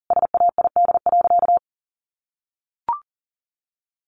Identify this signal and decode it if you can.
Morse code, heard as a keyed tone, about 40 wpm, 700 Hz. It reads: SAID.
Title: Sentences from Top 700 Words 40wpm